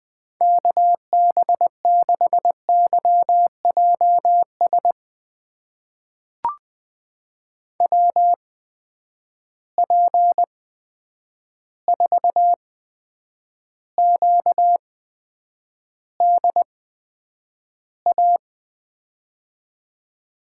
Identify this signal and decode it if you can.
Morse code, heard as a keyed tone, about 20 wpm, spaced as for 9 wpm, 700 Hz. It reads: KB6YJS WP4QDA
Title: USA Call Signs - 6 Characters - 8x Character Spacing - 20wpm